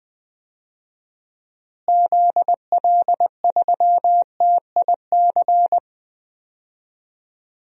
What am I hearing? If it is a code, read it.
Morse code, heard as a keyed tone, about 20 wpm, 700 Hz. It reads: ZL3TIC
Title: DXCC-100 Call Signs - 6 Characters - Repeated 2x - 20wpm